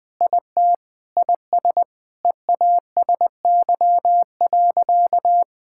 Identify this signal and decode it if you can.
Morse code, heard as a keyed tone, about 21 wpm, 700 Hz. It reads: IT IS EASY.
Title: Sentences with 7-Word Phrases - 20wpm